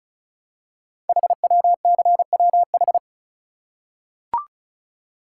Morse code, heard as a keyed tone, about 36 wpm, 690 Hz. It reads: HWCWH
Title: Code Groups - 5 Letters - 35wpm